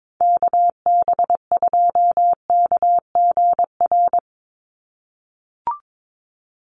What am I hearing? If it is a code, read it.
Morse code, heard as a keyed tone, about 22 wpm, 700 Hz. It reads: KB2KGR
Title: USA Call Signs 22wpm